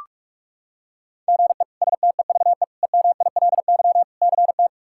Morse code, heard as a keyed tone, about 46 wpm, 700 Hz. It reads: GE STEVE EMILY CT
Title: K1USN SST QSO Practice - 45wpm